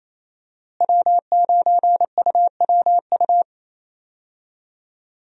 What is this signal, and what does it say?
Morse code, heard as a keyed tone, about 27 wpm, 700 Hz. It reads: W9UWU
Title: USA Call Signs - 5 Characters - Repeated 2x - 28wpm